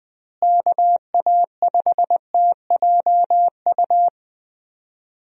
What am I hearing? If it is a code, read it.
Morse code, heard as a keyed tone, about 20 wpm, 700 Hz. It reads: KA5TJU
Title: USA Call Signs - 6 Characters - Repeated 2x - 20wpm